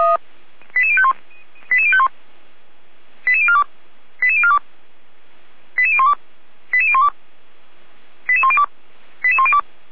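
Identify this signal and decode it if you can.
Five-tone selective call (ZVEI), heard as quick runs of five tones, tones of about 70 ms. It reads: 88051 88051 88042 88042 88012 88012 88102 88102
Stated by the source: ZVEI five-tone call received on SDR.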